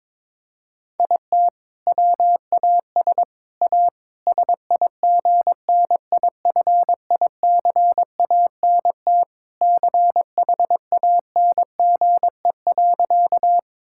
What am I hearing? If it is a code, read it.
Morse code, heard as a keyed tone, about 22 wpm, 700 Hz. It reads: IT WAS A SIGNIFICANT CHANGE.
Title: Sentences from Top 1000 Words - Review Part 6 - 22wpm